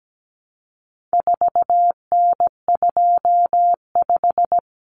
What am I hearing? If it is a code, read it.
Morse code, heard as a keyed tone, about 17 wpm, 700 Hz. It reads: 4N25
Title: Common Cut Numbers - 4-Digit 17wpm